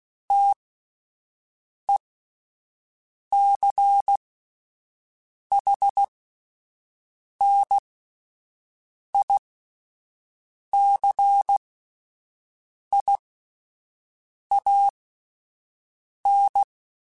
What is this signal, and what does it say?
Morse code, heard as a keyed tone, about 16 wpm, spaced as for 5 wpm, 800 Hz. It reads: TECHNICIAN
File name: fromSMs / FromSMs mp3s / Classes@5